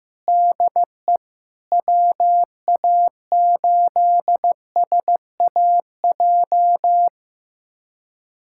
Morse code, heard as a keyed tone, about 15 wpm, 700 Hz. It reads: DE WA8SAJ
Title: USA Callsign DE Callsign - 6 Characters - 15wpm